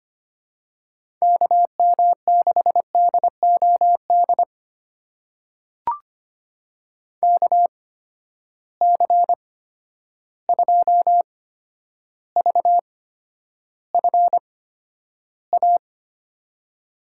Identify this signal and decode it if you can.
Morse code, heard as a keyed tone, about 25 wpm, spaced as for 11 wpm, 700 Hz. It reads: KM6DOD KC2VFA
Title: USA Call Signs - 6 Characters - 8x Character Spacing - 25wpm